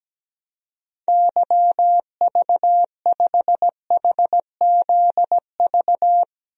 Morse code, heard as a keyed tone, about 17 wpm, 700 Hz. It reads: YV5HZV
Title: DXCC-100 Call Signs - 6 Characters - 17wpm